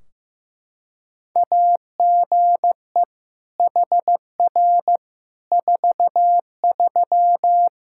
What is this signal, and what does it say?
Morse code, heard as a keyed tone, about 15 wpm, 700 Hz. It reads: AGE HR 43
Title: K1USN SST QSO Practice 15wpm